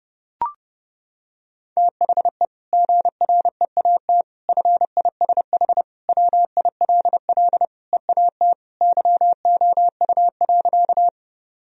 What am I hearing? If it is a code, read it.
Morse code, heard as a keyed tone, about 30 wpm, 700 Hz. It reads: THE GREAT FISH WILL EAT YOU.
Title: Sentences from Top 1000 Words - Review Part 1 - 30wpm